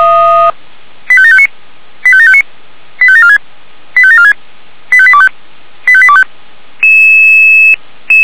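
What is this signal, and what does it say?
Five-tone selective call (ZVEI), heard as quick runs of five tones, tones of about 70 ms. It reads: 85759 85759 85746 85746 85825 85825
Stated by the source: ZVEI five-tone call 85728 received on SDR.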